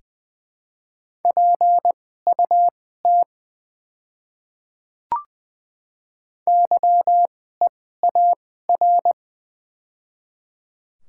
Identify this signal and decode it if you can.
Morse code, heard as a keyed tone, about 20 wpm, spaced as for 14 wpm, 700 Hz. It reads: PUT YEAR
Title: 2x Character Spacing - Top 200 Words - 20wpm